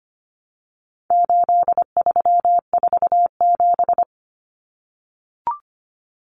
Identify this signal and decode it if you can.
Morse code, heard as a keyed tone, about 25 wpm, 700 Hz. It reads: 8347
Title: Numbers - 4-Digit 25wpm